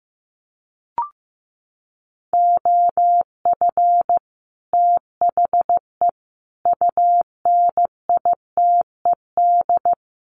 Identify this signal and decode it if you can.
Morse code, heard as a keyed tone, about 15 wpm, 700 Hz. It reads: OF THE UNITED
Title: Sets of 5 Words Encore 15wpm